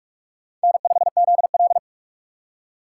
Morse code, heard as a keyed tone, about 44 wpm, 700 Hz. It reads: N5ZL